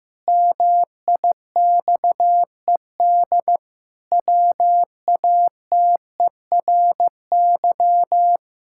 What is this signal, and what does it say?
Morse code, heard as a keyed tone, about 15 wpm, 700 Hz. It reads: MIXED WATERY
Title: Pangram Challenge v2 - 15wpm